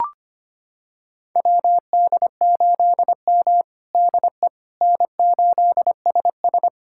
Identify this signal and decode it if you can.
Morse code, heard as a keyed tone, about 25 wpm, 700 Hz. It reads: WD8M DE N8SS
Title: USA Callsign DE Callsign - 4 Characters - 25wpm